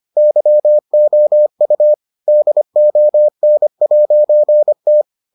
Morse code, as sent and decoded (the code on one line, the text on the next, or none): -.-- --- ..- / -.. --- -. .----. -
YOU DON'T